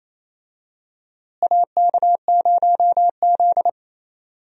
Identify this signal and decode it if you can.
Morse code, heard as a keyed tone, about 28 wpm, 700 Hz. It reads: AK0Z
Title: USA Call Signs - 4 Characters - Repeated 2x - 28wpm